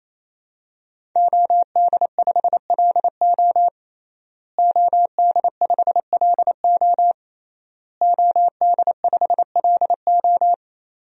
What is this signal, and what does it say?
Morse code, heard as a keyed tone, about 28 wpm, 700 Hz. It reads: OD5LO OD5LO OD5LO
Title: DXCC-100 Call Signs - 5 Characters - Repeated 3x - 28wpm